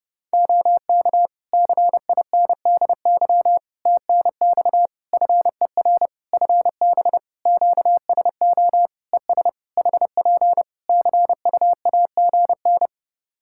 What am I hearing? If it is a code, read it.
Morse code, heard as a keyed tone, about 30 wpm, 700 Hz. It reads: OK CINDY TNX FER FB QSO ES HP CUAGN
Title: K1USN SST QSO Practice 30wpm